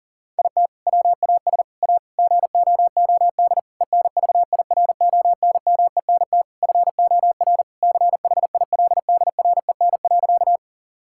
Morse code, heard as a keyed tone, about 40 wpm, 700 Hz. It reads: IT WAS A GOOD ENVIRONMENT FOR CHILDREN.
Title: Sentences from Top 1000 Words - Review Part 5 - 40wpm